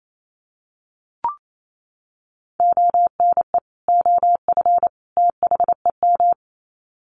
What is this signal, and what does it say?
Morse code, heard as a keyed tone, about 28 wpm, 700 Hz. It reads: ONE OF THEM
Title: Sets of 3 Words 28wpm